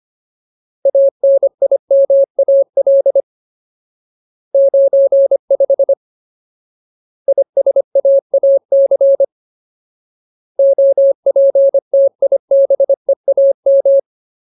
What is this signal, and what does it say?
Morse code, heard as a keyed tone, about 25 wpm, 550 Hz. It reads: ANIMAL 95 ISAAC OPTIBEAM
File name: Ultimate Word List / 25WPM Ultimate